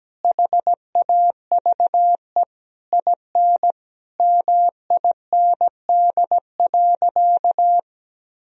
Morse code, heard as a keyed tone, about 17 wpm, 700 Hz. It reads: HAVE IN MIND.
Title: Sentences from Top 1000 Words - Review Part 3 - 17wpm